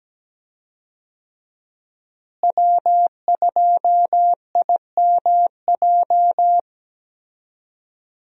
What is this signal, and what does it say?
Morse code, heard as a keyed tone, about 17 wpm, 700 Hz. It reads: W2IMJ
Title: USA Call Signs - 5 Characters - Repeated 3x - 17wpm